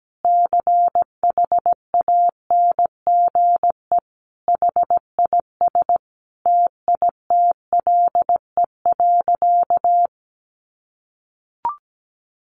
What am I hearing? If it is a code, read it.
Morse code, heard as a keyed tone, about 17 wpm, 700 Hz. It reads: CHANGE HIS TITLE.
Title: Sentences from Top 1200 Words 17wpm